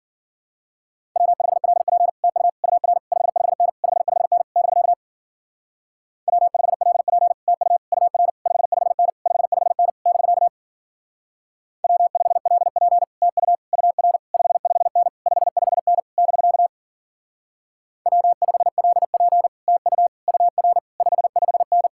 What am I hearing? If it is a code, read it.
Morse code, heard as a keyed tone, about 50 wpm, 700 Hz. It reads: W5LP TU UR 55N 55N <BK> W5LP TU UR 55N 55N <BK> W5LP TU UR 55N 55N <BK> W5LP TU UR 55N
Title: Speed-Racing Common QSO Phrases - 40wpm